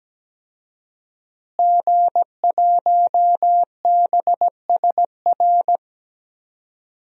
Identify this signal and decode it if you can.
Morse code, heard as a keyed tone, about 17 wpm, 700 Hz. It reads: G1BSR
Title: DXCC-100 Call Signs - 5 Characters - Repeated 3x - 17wpm